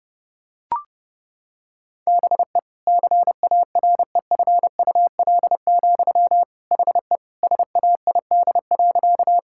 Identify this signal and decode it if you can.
Morse code, heard as a keyed tone, about 30 wpm, 700 Hz. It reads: BE CAREFUL, HE SAID.
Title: Sentences from Top 1600 Words 30wpm